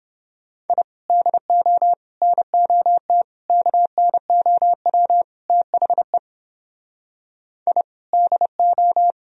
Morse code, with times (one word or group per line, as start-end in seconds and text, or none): 0.7-0.8 I
1.1-1.9 DO
2.2-3.2 NOT
3.5-5.2 KNOW
5.5-6.2 THE
7.7-7.8 I
8.1-9.1 DO